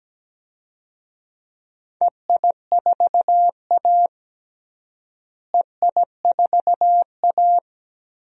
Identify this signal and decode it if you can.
Morse code, heard as a keyed tone, about 17 wpm, 700 Hz. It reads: EI4A EI4A
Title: DXCC-100 Call Signs - 4 Characters - Repeated 3x - 17wpm